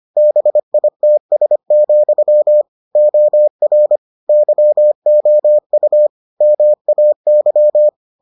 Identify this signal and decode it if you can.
Morse code, heard as a keyed tone, about 25 wpm, 600 Hz. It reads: BITS, OR YOU MAY